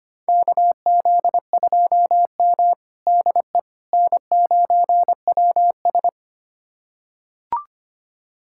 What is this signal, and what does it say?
Morse code, heard as a keyed tone, about 25 wpm, 700 Hz. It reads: KZ2M DE N9WS
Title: USA Callsign DE Callsign - 4 Characters - 25wpm